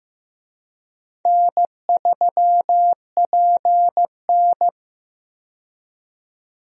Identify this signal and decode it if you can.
Morse code, heard as a keyed tone, about 15 wpm, 700 Hz. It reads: N3PN
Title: USA Call Signs - 4 Characters - 15wpm